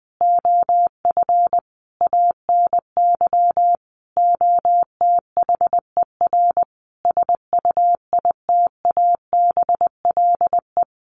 Morse code, heard as a keyed tone, about 20 wpm, 700 Hz. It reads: OF ANY OTHER SUITABLE
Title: Sentences from Top 1400 Words 20wpm